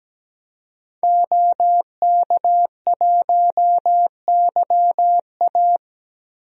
Morse code, heard as a keyed tone, about 17 wpm, 700 Hz. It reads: OK1YA
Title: DXCC-100 Call Signs - 5 Characters - Repeated 2x - 17wpm